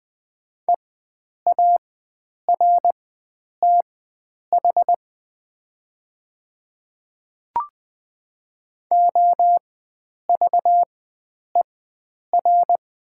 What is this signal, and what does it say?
Morse code, heard as a keyed tone, about 20 wpm, spaced as for 9 wpm, 700 Hz. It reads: EARTH OVER
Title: 4x Character Spacing - Top 500 Words - 20wpm